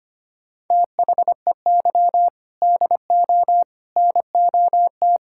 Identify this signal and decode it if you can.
Morse code, heard as a keyed tone, about 25 wpm, 700 Hz. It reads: THEY DO NOT
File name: practice / 25wpm / Sentences with 4-Word Phrases - 25wpm